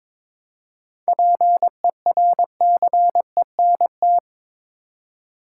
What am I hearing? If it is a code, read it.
Morse code, heard as a keyed tone, about 21 wpm, 700 Hz. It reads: PERCENT
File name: Build A Phrase - 2-Word Phrases - 22wpm